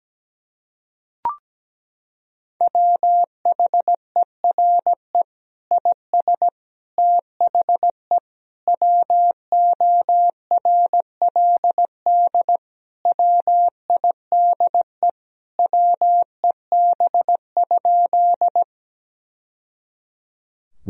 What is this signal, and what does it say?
Morse code, heard as a keyed tone, about 17 wpm, 700 Hz. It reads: WHERE IS THE WORLD WIDE WEB?
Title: Sentences from Top 1000 Words - Review Part 4 - 17wpm